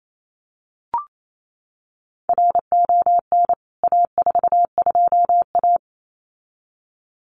Morse code, heard as a keyed tone, about 28 wpm, 700 Hz. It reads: RON A42A
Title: CWT Exchange 28wpm